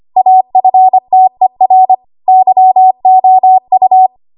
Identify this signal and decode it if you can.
Morse code, heard as a keyed tone, about 25 wpm, 750 Hz. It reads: AFTER YOU
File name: Archive / 25 WPM / 25WPM